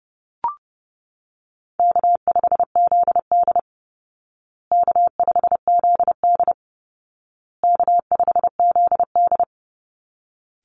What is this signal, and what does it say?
Morse code, heard as a keyed tone, about 30 wpm, 700 Hz. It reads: K5ZD K5ZD K5ZD
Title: Easy Call Signs Repeated 3x - 30wpm